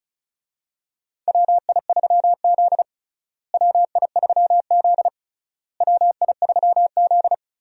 Morse code, heard as a keyed tone, about 36 wpm, 700 Hz. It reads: WI3Z WI3Z WI3Z
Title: USA Call Signs - 4 Characters - Repeated 3x - 35wpm